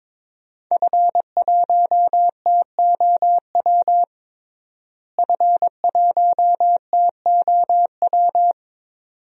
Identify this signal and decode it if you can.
Morse code, heard as a keyed tone, about 22 wpm, 700 Hz. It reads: F1TOW F1TOW
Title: DXCC-100 Call Signs - 5 Characters - Repeated 2x - 22wpm